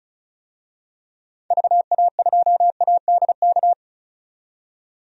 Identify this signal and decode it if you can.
Morse code, heard as a keyed tone, about 36 wpm, 700 Hz. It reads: VA2ADK
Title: DXCC-100 Call Signs - 6 Characters - 35wpm